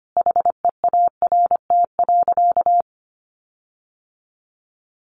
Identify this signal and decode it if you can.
Morse code, heard as a keyed tone, about 25 wpm, 700 Hz. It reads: HEART.
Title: Sentences from Top 1500 Words 25wpm